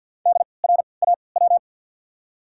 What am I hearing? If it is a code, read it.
Morse code, heard as a keyed tone, about 54 wpm, spaced as for 29 wpm, 690 Hz. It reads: DRAW
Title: 3x Character Spacing - Top 300 Words - 50wpm